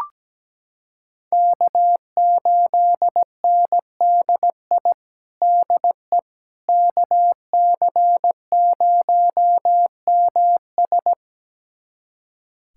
Morse code, heard as a keyed tone, about 17 wpm, 700 Hz. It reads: K8NDI DE KC0MS
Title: USA Callsign DE Callsign - 5 Characters - 17wpm